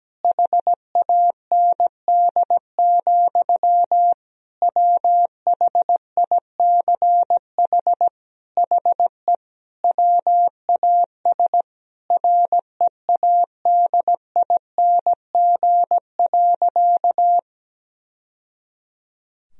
Morse code, heard as a keyed tone, about 17 wpm, 700 Hz. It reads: HAND, WHICH HE WAS READING.
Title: Sentences from Top 1000 Words - Review Part 3 - 17wpm